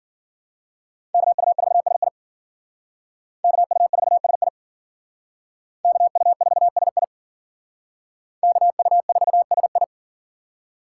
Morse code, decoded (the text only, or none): KU4SI KU4SI KU4SI KU4SI